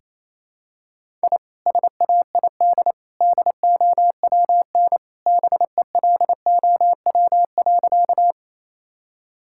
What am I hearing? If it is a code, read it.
Morse code, heard as a keyed tone, about 28 wpm, 700 Hz. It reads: I SAID DOWN BELOW.
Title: Sentences from Top 1000 Words - Review Part 5 - 28wpm